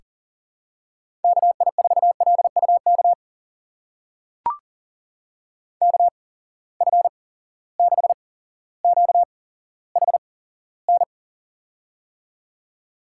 Morse code, heard as a keyed tone, about 40 wpm, spaced as for 17 wpm, 700 Hz. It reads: KI4LUK KF6QHN
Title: USA Call Signs - 8x Character Spacing - 40wpm